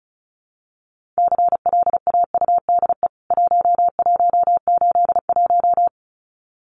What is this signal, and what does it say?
Morse code, heard as a keyed tone, about 34 wpm, 700 Hz. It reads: CLAUDE 1181
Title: CWT Exchange 35wpm